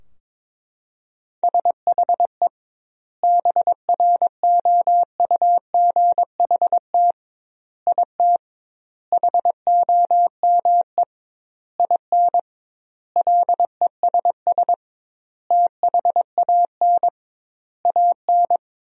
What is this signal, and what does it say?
Morse code, heard as a keyed tone, about 21 wpm, 700 Hz. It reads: SHE BROUGHT IT HOME IN LESS THAN AN
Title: 2x Word-Spacing - Sentences from Top 500 Words - 22wpm